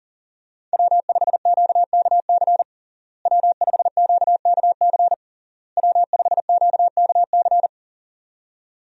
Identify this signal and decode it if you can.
Morse code, heard as a keyed tone, about 37 wpm, 700 Hz. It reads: W5QKC W5QKC W5QKC
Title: USA Call Signs - 5 Characters - Repeated 3x - 40wpm